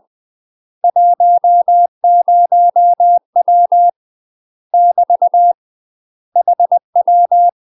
Morse code, heard as a keyed tone, about 20 wpm, 700 Hz. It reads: 10W <BT> HW
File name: UK MASTER QSOs TRAINING / 20WPM / UK MASTER QSOs 0060